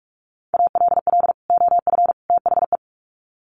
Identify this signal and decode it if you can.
Morse code, heard as a keyed tone, about 44 wpm, 700 Hz. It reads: ALL OF THE
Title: Sets of 3 Words 45wpm